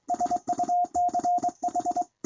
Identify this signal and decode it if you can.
Morse code, heard as a keyed tone, about 22 wpm, 710 Hz. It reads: SUCH
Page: NW6V sent me a recording of his doing 25 wpm on a straight key.